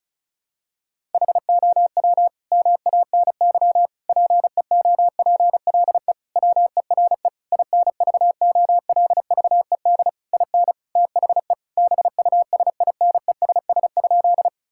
Morse code, heard as a keyed tone, about 36 wpm, 700 Hz. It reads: HOW MANY PEOPLE WERE INVOLVED IN THE BUSINESS?
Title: Sentences from Top 1000 Words - Review Part 5 - 35wpm